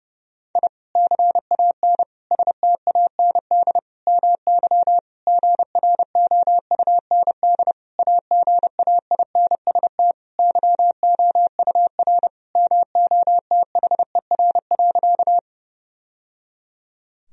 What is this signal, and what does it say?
Morse code, heard as a keyed tone, about 30 wpm, 700 Hz. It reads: I CAN STAND MY GROUND AGAINST YOUR MOTHER.
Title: Sentences from Top 1000 Words - Review Part 2 - 30wpm